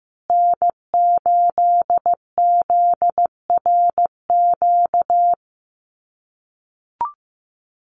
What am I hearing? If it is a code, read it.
Morse code, heard as a keyed tone, about 15 wpm, 700 Hz. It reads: N8ZRQ
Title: USA Call Signs Repeated 3x 15wpm